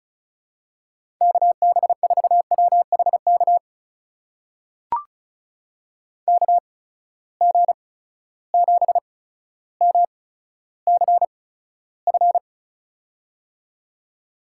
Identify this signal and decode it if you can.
Morse code, heard as a keyed tone, about 36 wpm, spaced as for 15 wpm, 700 Hz. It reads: KB4WHK KG7MCF
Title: USA Call Signs - 6 Characters - 8x Character Spacing - 35wpm